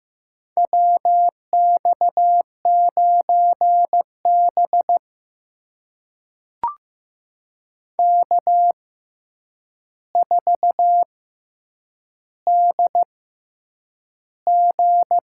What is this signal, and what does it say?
Morse code, heard as a keyed tone, about 15 wpm, spaced as for 8 wpm, 700 Hz. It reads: WX9B K4DG
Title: USA Call Signs - 4 Characters - 6x Character Spacing - 15wpm